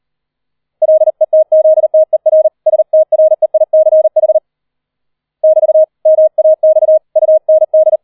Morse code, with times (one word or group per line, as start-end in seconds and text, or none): 0.8-2.5 LETZTER
2.7-4.4 STREICH
5.4-5.8 <BT>
6.1-7.0 MAX
7.2-8.0 UND